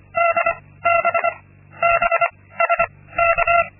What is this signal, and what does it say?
Morse code, heard as a keyed tone, about 25 wpm, 680 Hz. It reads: D B B S K
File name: Linksmakalnis-2